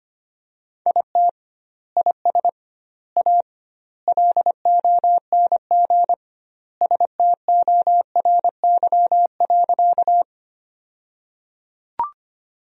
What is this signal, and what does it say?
Morse code, heard as a keyed tone, about 25 wpm, 700 Hz. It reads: IT IS A LONG STORY.
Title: 2x Word-Spacing - Sentences from Top 300 Words - 25wpm